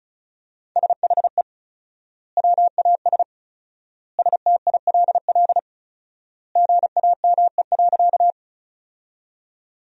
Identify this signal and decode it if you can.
Morse code, heard as a keyed tone, about 34 wpm, 700 Hz. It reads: SHE WAS STILL GAME.
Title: 4x Word-Spacing - Sentences from Top 500 Words - 35wpm